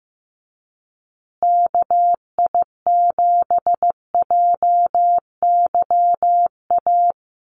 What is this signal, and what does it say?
Morse code, heard as a keyed tone, about 15 wpm, 700 Hz. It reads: KI7JYA
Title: USA Call Signs Repeated 3x 15wpm